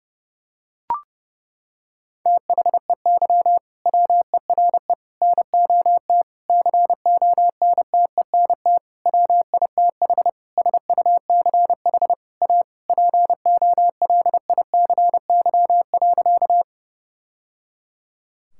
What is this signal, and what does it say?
Morse code, heard as a keyed tone, about 30 wpm, 700 Hz. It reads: THEY WERE NOT CONTENT WITH SUCH A POLICY.
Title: Sentences from Top 1000 Words - Review Part 6 - 30wpm